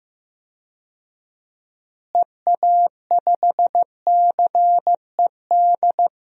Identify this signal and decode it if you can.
Morse code, heard as a keyed tone, about 15 wpm, 700 Hz. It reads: EA5CED
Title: DXCC-100 Call Signs - 6 Characters - Repeated 3x - 15wpm